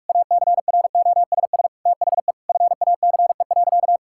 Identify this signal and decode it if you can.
Morse code, heard as a keyed tone, about 41 wpm, 700 Hz. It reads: ACROSS THE FACE.
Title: Sentences from Top 1000 Words - Review Part 5 - 45wpm